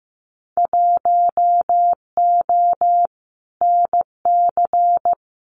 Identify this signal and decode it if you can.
Morse code, heard as a keyed tone, about 15 wpm, 700 Hz. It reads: 1O NC
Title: Winter Field Day Exchanges 15wpm